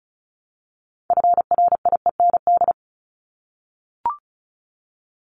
Morse code, heard as a keyed tone, about 34 wpm, 700 Hz. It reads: FRIEND
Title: Top 500 Words 35wpm